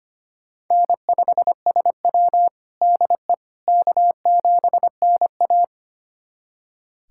Morse code, heard as a keyed tone, about 25 wpm, 700 Hz. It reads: N5SW DE K7NA
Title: USA Callsign DE Callsign - 4 Characters - 25wpm